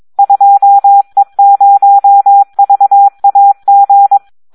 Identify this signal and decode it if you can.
Morse code, heard as a keyed tone, about 22 wpm, 800 Hz. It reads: 2E0VAG
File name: Audio practica CW / SD 1 22